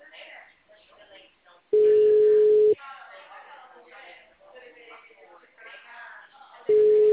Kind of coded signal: telephone call progress tone

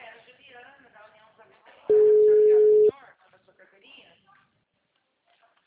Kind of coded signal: telephone call progress tone

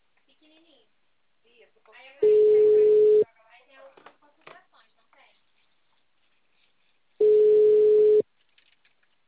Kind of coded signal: telephone call progress tone

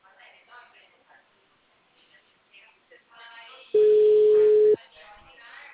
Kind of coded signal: telephone call progress tone

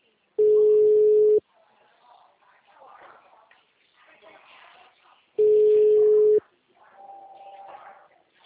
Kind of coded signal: telephone call progress tone

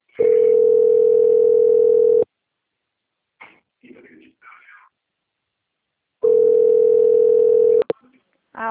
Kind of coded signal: telephone call progress tone

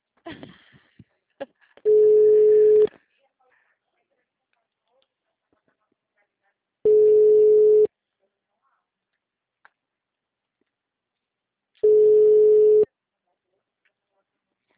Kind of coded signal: telephone call progress tone

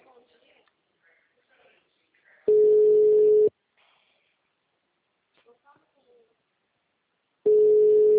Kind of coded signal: telephone call progress tone

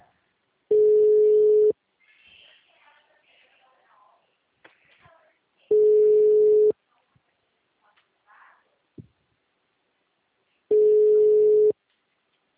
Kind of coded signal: telephone call progress tone

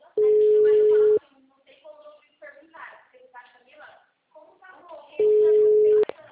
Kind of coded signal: telephone call progress tone

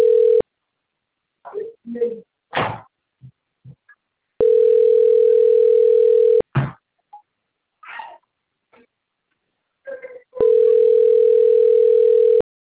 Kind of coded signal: telephone call progress tone

